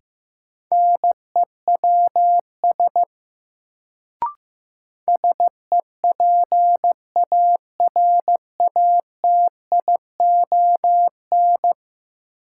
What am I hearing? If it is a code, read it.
Morse code, heard as a keyed tone, about 15 wpm, 700 Hz. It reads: NEWS SEPARATION
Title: Sets of 5 Words - New Word Drill - 15wpm